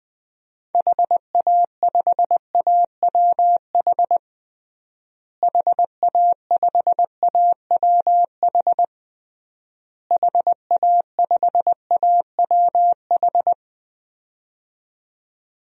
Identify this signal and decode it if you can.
Morse code, heard as a keyed tone, about 20 wpm, 700 Hz. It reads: HA5AWH HA5AWH HA5AWH
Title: DXCC-100 Call Signs - 6 Characters - Repeated 3x - 20wpm